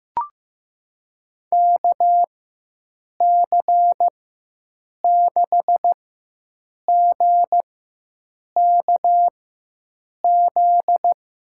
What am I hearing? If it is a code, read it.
Morse code, heard as a keyed tone, about 15 wpm, spaced as for 7 wpm, 700 Hz. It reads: KC6GKZ
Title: USA Call Signs - 6 Characters - 4x Character Spacing - 15wpm